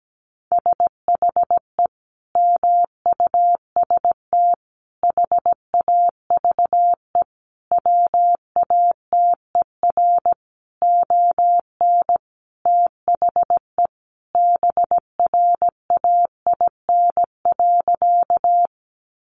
Morse code, heard as a keyed tone, about 17 wpm, 700 Hz. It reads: SHE MUST HAVE WATER ON THE BRAIN.
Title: Sentences from Top 1100 Words 17wpm